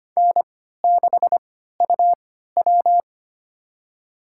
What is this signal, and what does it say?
Morse code, heard as a keyed tone, about 25 wpm, spaced as for 14 wpm, 700 Hz. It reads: N6UW
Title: USA Call Signs - 4 Characters - 3x Character Spacing - 25wpm